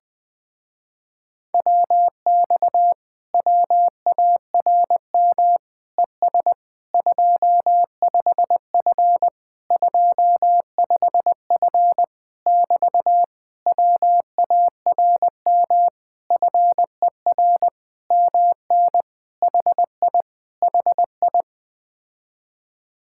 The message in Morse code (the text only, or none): WX WARM ES 25F 25F <BT> WARM FER MN HI HI